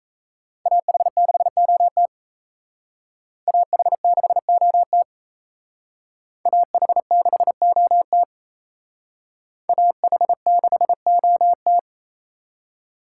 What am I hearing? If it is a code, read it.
Morse code, heard as a keyed tone, about 32 wpm, 700 Hz. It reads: AH6OT AH6OT AH6OT AH6OT
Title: Speed Racing - USA 2x2 Call Signs 28wpm